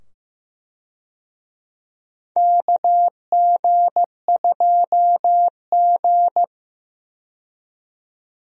Morse code, heard as a keyed tone, about 15 wpm, 700 Hz. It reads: KG2G
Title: USA Call Signs - 4 Characters - 15wpm